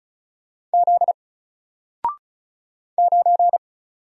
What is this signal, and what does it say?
Morse code, heard as a keyed tone, about 36 wpm, 700 Hz. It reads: Z 9
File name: Single-Letter-Number-35wpm